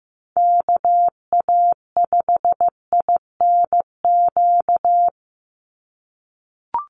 Morse code, heard as a keyed tone, about 15 wpm, 700 Hz. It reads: KA5INQ
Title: USA Call Signs 15wpm